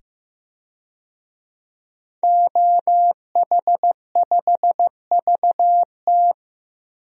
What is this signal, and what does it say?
Morse code, heard as a keyed tone, about 15 wpm, 700 Hz. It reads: OH5VT
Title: DXCC-100 Call Signs - 5 Characters - 15wpm